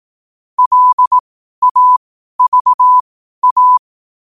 Morse code, heard as a keyed tone, about 18 wpm, spaced as for 12 wpm, 1000 Hz. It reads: LAVA
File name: qotdfiles / 12wpm